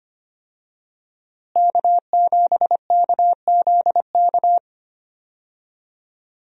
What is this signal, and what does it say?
Morse code, heard as a keyed tone, about 25 wpm, 700 Hz. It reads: K7KZK
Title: USA Call Signs - 5 Characters - Repeated 2x - 25wpm